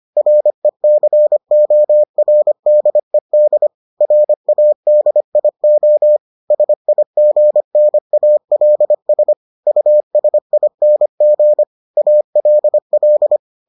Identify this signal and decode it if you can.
Morse code, heard as a keyed tone, about 25 wpm, 600 Hz. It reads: RECORDED RADIO SIGNALS USING ALL